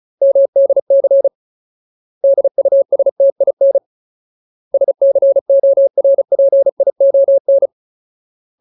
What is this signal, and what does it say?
Morse code, heard as a keyed tone, about 34 wpm, 550 Hz. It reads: MDC DUSTIN SCORPION
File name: Ultimate Word List / 35WPM Ultimate